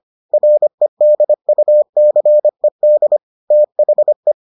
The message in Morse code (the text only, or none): REDUCED THE